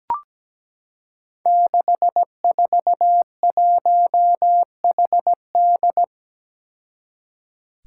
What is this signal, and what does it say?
Morse code, heard as a keyed tone, about 17 wpm, 700 Hz. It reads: 641HD
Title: Five Challenging Letter-Number Combinations 17wpm